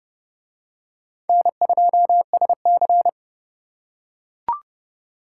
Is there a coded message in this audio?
Yes: Morse code from 1.3 to 3.1 s, heard as a keyed tone, about 28 wpm, 700 Hz. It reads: N2SC